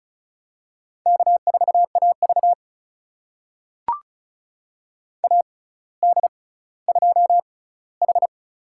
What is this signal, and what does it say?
Morse code, heard as a keyed tone, about 36 wpm, spaced as for 18 wpm, 700 Hz. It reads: K4AV AD2H
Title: USA Call Signs - 4 Characters - 6x Character Spacing - 35wpm